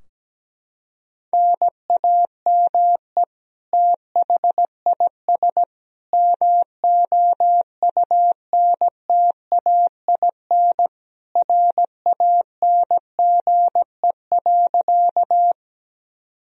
Morse code, heard as a keyed tone, about 17 wpm, 700 Hz. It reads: NAME THIS MOUNTAIN RANGE.
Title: Sentences from Top 1000 Words - Review Part 4 - 17wpm